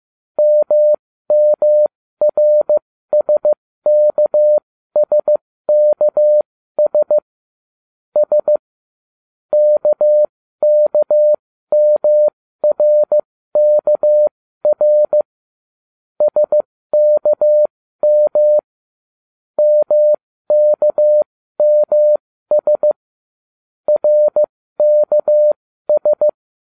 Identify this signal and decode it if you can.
Morse code, heard as a keyed tone, about 15 wpm, spaced as for 13 wpm, 600 Hz. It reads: MMRSKSKS S KKMRKR SKM MKMS RKS